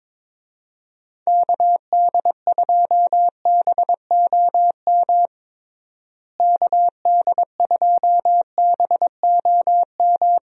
Morse code, heard as a keyed tone, about 22 wpm, 700 Hz. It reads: KD2BOM KD2BOM
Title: USA Call Signs - 6 Characters - Repeated 3x - 22wpm